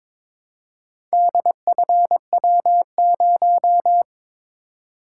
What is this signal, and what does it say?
Morse code, heard as a keyed tone, about 22 wpm, 700 Hz. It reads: DFW0
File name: practice / 22wpm / Four Letter-Number Combinations - 22wpm